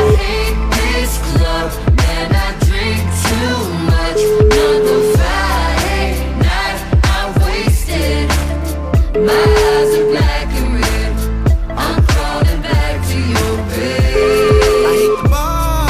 Ringback tone (Europe, 425 Hz), with noise.